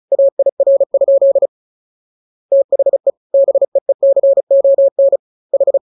Morse code, heard as a keyed tone, about 34 wpm, 550 Hz. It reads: AIR? THE BEECON H
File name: jokes / joke 28 35 0000